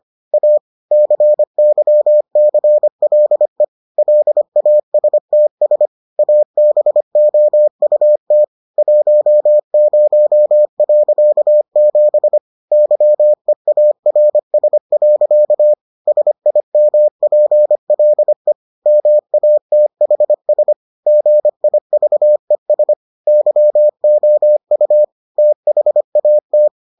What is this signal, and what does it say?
Morse code, heard as a keyed tone, about 25 wpm, 600 Hz. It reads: A CYCLE LASTS ABOUT 10.7 YEARS. SIMPLE MATHS GIVES YOU THAT